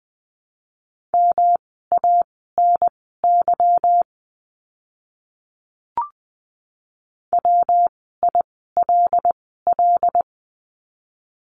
Morse code, heard as a keyed tone, about 20 wpm, spaced as for 14 wpm, 700 Hz. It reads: MANY WILL
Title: 237 Lesson 38f - Letter q - Words 20wpm